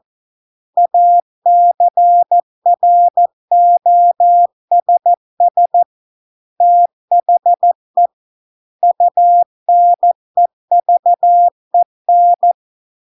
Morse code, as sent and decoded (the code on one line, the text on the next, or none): .- -.-. .-. --- ... ... / - .... / ..- -. ...- . -.
ACROSS THE UNEVEN